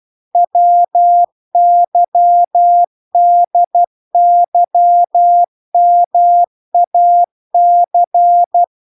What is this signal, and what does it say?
Morse code, heard as a keyed tone, about 12 wpm, 700 Hz. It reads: WYDYMAC